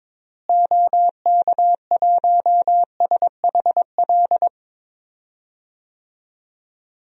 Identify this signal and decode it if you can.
Morse code, heard as a keyed tone, about 22 wpm, 700 Hz. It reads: OK1SHL
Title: DXCC-100 Call Signs - 6 Characters - Repeated 3x - 22wpm